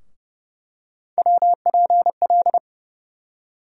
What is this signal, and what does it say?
Morse code, heard as a keyed tone, about 30 wpm, 700 Hz. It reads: WPL
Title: USA Call Sign Postfixes - 3 Letters 30wpm